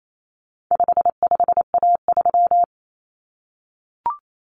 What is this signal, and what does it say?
Morse code, heard as a keyed tone, about 28 wpm, 700 Hz. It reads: 55A3
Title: Common Cut Numbers - 4-Digit 28wpm